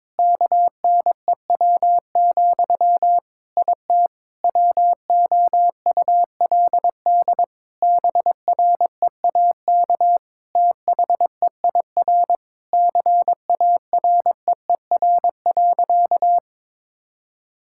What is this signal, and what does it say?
Morse code, heard as a keyed tone, about 22 wpm, 700 Hz. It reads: KNEW, IT WOULD BREAK THEIR CAREER.
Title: Sentences from Top 1000 Words - Review Part 6 - 22wpm